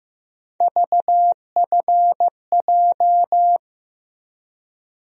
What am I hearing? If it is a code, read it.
Morse code, heard as a keyed tone, about 15 wpm, 700 Hz. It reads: VFJ
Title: USA Call Sign Postfixes - 3 Letters 15wpm